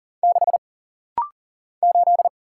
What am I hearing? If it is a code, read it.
Morse code, heard as a keyed tone, about 40 wpm, 700 Hz. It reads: 6 8
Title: Single Letter-Number - Rapid-Fire 40wpm